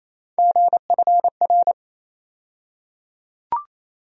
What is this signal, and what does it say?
Morse code, heard as a keyed tone, about 28 wpm, 700 Hz. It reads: GFR
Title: USA Call Sign Postfixes - 3 Letters 28wpm